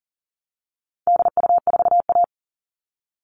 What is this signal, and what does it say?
Morse code, heard as a keyed tone, about 40 wpm, 700 Hz. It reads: DU4A
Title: All Cut Numbers - 4-Digit 40wpm